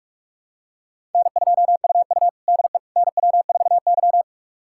Morse code, heard as a keyed tone, about 43 wpm, 700 Hz. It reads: N2UU DE NW4Y
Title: USA Callsign DE Callsign - 4 Characters - 45wpm